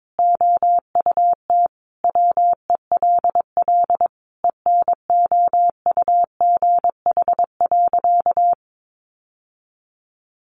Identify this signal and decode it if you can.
Morse code, heard as a keyed tone, about 22 wpm, 700 Hz. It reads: OUT WELL ENOUGH.
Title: Sentences from Top 600 Words 22wpm